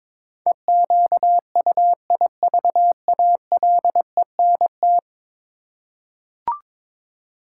Morse code, heard as a keyed tone, about 22 wpm, 700 Hz. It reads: EQUIVALENT
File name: practice / 22wpm / 10 Letter Words - First 3 Letters Given - 22wpm